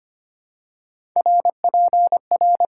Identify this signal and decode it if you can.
Morse code, heard as a keyed tone, about 25 wpm, 690 Hz. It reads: R P R
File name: Top 90 QSO Elements - 6x Character Spacing - 25wpm